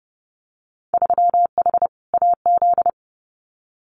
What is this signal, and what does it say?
Morse code, heard as a keyed tone, about 32 wpm, 700 Hz. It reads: 3H AZ
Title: Winter Field Day Exchanges Encore 30wpm